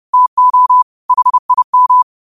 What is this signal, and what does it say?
Morse code, heard as a keyed tone, about 30 wpm, 1000 Hz. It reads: TO HIM
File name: qotdfiles / 30wpm